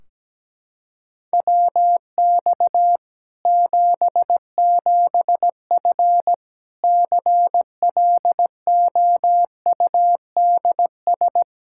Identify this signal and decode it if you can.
Morse code, heard as a keyed tone, about 17 wpm, 700 Hz. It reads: WX 77F CLOUDS
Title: Practice QSO - Exchange 2 - Info - 17wpm